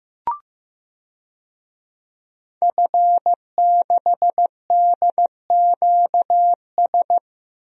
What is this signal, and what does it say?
Morse code, heard as a keyed tone, about 15 wpm, 700 Hz. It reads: F6DQS
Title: DXCC-100 Call Signs - 5 Characters - 15wpm